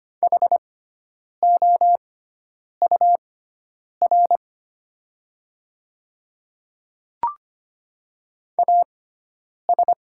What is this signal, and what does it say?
Morse code, heard as a keyed tone, about 25 wpm, spaced as for 9 wpm, 700 Hz. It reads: HOUR AS
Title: 6x Character Spacing - Top 500 Words - 25wpm